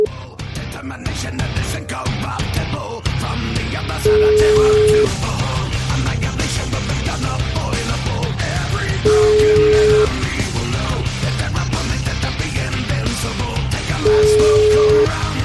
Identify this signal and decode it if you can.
Ringback tone (Europe, 425 Hz), with noise.